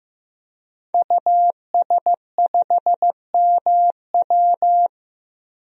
Morse code, heard as a keyed tone, about 15 wpm, 700 Hz. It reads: US5MW
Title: DXCC-100 Call Signs - 5 Characters - 15wpm